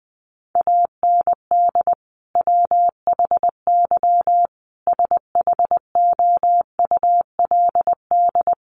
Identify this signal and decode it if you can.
Morse code, heard as a keyed tone, about 20 wpm, 700 Hz. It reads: AND WHY SHOULD
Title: Sentences from Top 300 Words 20wpm